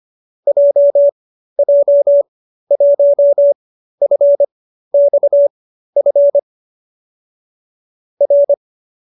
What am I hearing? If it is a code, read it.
Morse code, heard as a keyed tone, about 25 wpm, spaced as for 13 wpm, 570 Hz. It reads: JJ1FXF R
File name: qso208b 13